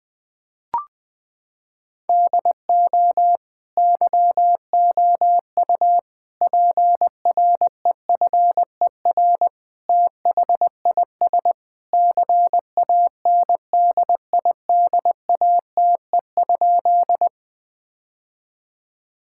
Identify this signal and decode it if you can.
Morse code, heard as a keyed tone, about 21 wpm, 700 Hz. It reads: DO YOU PREFER THIS CANDIDATE?
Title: Sentences from Top 1600 Words 20wpm